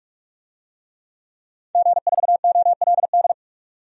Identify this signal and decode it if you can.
Morse code, heard as a keyed tone, about 46 wpm, 700 Hz. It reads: G4OLD
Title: DXCC-100 Call Signs - 5 Characters - 45wpm